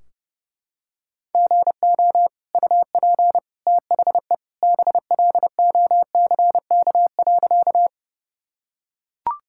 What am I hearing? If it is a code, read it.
Morse code, heard as a keyed tone, about 32 wpm, 700 Hz. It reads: GO UP THE BLOCK.
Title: Sentences from Top 1600 Words 30wpm